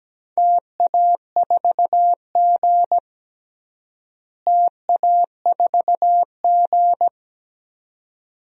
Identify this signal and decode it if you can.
Morse code, heard as a keyed tone, about 17 wpm, 700 Hz. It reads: TA4G TA4G
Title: DXCC-100 Call Signs - 4 Characters - Repeated 2x - 17wpm